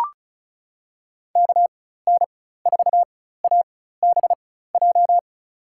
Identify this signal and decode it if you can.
Morse code, heard as a keyed tone, about 36 wpm, spaced as for 16 wpm, 700 Hz. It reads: KN4ABJ
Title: USA Call Signs - 6 Characters - 4x Character Spacing - 35wpm